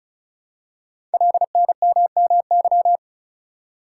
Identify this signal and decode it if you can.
Morse code, heard as a keyed tone, about 36 wpm, 690 Hz. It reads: LNMMY
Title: Code Groups - 5 Letters - 35wpm